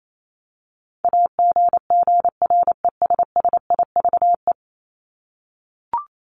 Morse code, heard as a keyed tone, about 28 wpm, 700 Hz. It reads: AGGRESSIVE
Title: Top 1501-1750 Words 28wpm